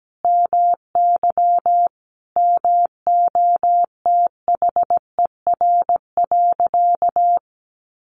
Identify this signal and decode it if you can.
Morse code, heard as a keyed tone, about 17 wpm, 700 Hz. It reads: MY MOTHER.
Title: Sentences from Top 1400 Words 17wpm